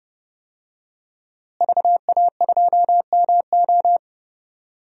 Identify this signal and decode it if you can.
Morse code, heard as a keyed tone, about 30 wpm, 700 Hz. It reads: VA2MO
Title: DXCC-100 Call Signs - 5 Characters - Repeated 3x - 30wpm